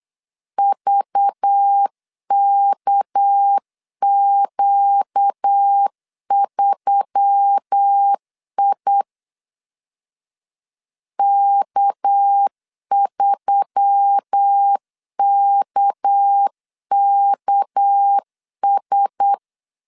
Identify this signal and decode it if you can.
Morse code, heard as a keyed tone, about 8 wpm, 800 Hz. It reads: VKQ3I K3KKS
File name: LEZIONI / 6 PKX0 / 6 PK 8wpm 3003